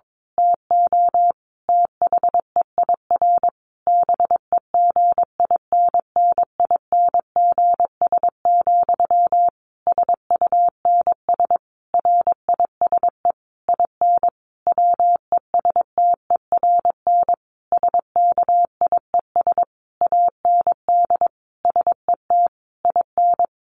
Morse code, heard as a keyed tone, about 22 wpm, 700 Hz. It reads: TO THEIR BEGINNINGS, SUNS RISE IN WESTERN SKIES AND SET IN
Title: A Sound Of Thunder 22wpm